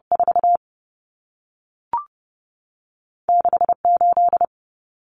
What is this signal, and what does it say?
Morse code, heard as a keyed tone, about 30 wpm, 700 Hz. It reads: 4 6 8
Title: Numbers - 2-Digit 30wpm